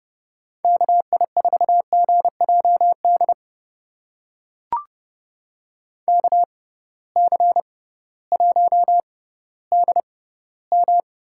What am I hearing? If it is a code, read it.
Morse code, heard as a keyed tone, about 30 wpm, spaced as for 16 wpm, 700 Hz. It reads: KI4GJD KC1DM
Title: USA Call Signs - 6 Characters - 6x Character Spacing - 30wpm